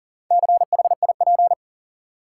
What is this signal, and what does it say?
Morse code, heard as a keyed tone, about 40 wpm, 700 Hz. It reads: CHIP
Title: Four Letter Words - 40wpm